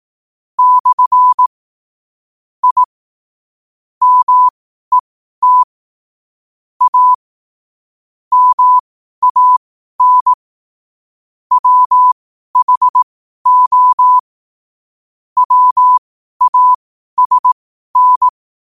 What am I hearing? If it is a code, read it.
Morse code, heard as a keyed tone, about 18 wpm, spaced as for 12 wpm, 1000 Hz. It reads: / I MET A MAN WHO WASN
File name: qotdfiles / 12wpm